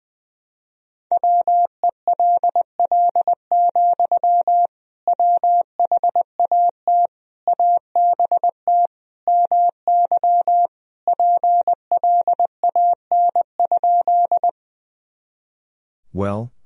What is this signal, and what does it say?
Morse code, heard as a keyed tone, about 20 wpm, 700 Hz. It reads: WELL, WHAT ABT MY PLAN?
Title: Sentences from Top 500 Words and Abbreviations 20wpm